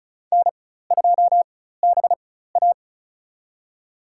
Morse code, heard as a keyed tone, about 34 wpm, spaced as for 16 wpm, 700 Hz. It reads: N2BA
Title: USA Call Signs - 4 Characters - 4x Character Spacing - 35wpm